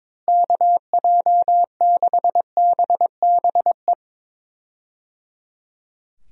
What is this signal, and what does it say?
Morse code, heard as a keyed tone, about 21 wpm, 700 Hz. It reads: KJ6BBE
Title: USA Call Signs - 6 Characters - Repeated 3x - 22wpm